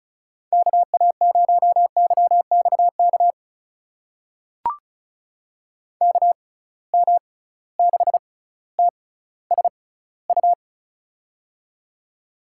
Morse code, heard as a keyed tone, about 36 wpm, spaced as for 18 wpm, 700 Hz. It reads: KA0YXK KM6TSU
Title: USA Call Signs - 6 Characters - 6x Character Spacing - 35wpm